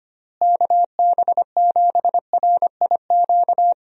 Morse code, heard as a keyed tone, about 25 wpm, 700 Hz. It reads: KB7RIQ
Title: USA Call Signs - 6 Characters - Repeated 3x - 25wpm